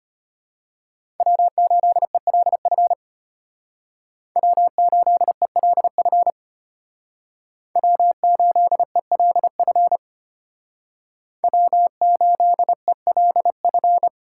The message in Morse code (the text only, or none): W8ELF W8ELF W8ELF W8ELF